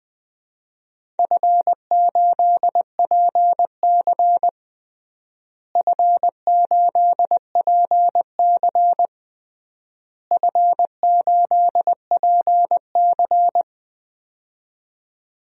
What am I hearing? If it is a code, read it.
Morse code, heard as a keyed tone, about 20 wpm, 700 Hz. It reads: F8PC F8PC F8PC
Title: DXCC-100 Call Signs - 4 Characters - Repeated 3x - 20wpm